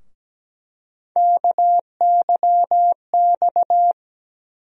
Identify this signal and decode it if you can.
Morse code, heard as a keyed tone, about 17 wpm, 700 Hz. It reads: KYX
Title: USA Call Sign Postfixes - 3 Letters 17wpm